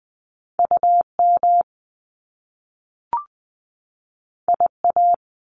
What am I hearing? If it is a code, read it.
Morse code, heard as a keyed tone, about 20 wpm, 700 Hz. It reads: U M I A
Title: Top Two Letter Combinations in Words 20wpm